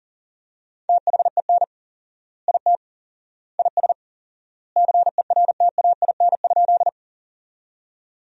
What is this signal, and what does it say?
Morse code, heard as a keyed tone, about 37 wpm, 700 Hz. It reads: THEN IT IS CERTAIN?
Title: 4x Word-Spacing - Sentences from Top 500 Words - 40wpm